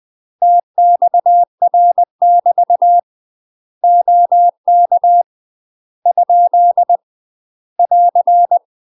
Morse code, heard as a keyed tone, about 20 wpm, 700 Hz. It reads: TXR <BT> OK ? <AR>
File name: UK MASTER QSOs TRAINING / 20WPM / UK MASTER QSOs 0057